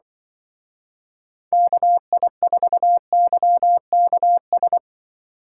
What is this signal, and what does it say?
Morse code, heard as a keyed tone, about 25 wpm, 700 Hz. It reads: KI4YKS